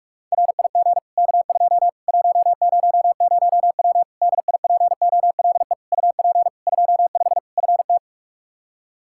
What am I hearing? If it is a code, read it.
Morse code, heard as a keyed tone, about 42 wpm, 700 Hz. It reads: RIG K2 100W DIPOLE UP 25 FT
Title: Practice QSO - Exchange 2 - Info - 45wpm